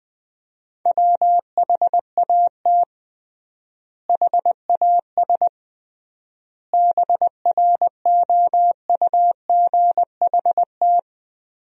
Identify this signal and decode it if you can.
Morse code, heard as a keyed tone, about 20 wpm, 700 Hz. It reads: WHAT HAS BROUGHT
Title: 3x Word-Spacing - Sentences from Top 500 Words - 20wpm